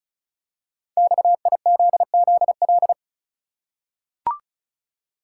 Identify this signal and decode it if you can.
Morse code, heard as a keyed tone, about 36 wpm, 690 Hz. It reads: XIZZL
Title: Code Groups - 5 Letters - 35wpm